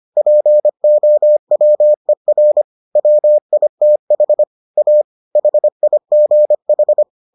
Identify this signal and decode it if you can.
Morse code, heard as a keyed tone, about 25 wpm, 600 Hz. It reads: POWER WITH A HIGH